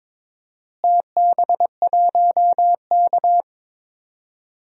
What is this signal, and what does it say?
Morse code, heard as a keyed tone, about 22 wpm, 700 Hz. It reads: TB1K
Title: DXCC-100 Call Signs - 4 Characters - Repeated 2x - 22wpm